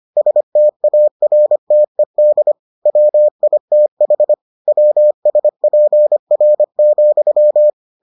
Morse code, heard as a keyed tone, about 25 wpm, 600 Hz. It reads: STARTED WITH WSPR,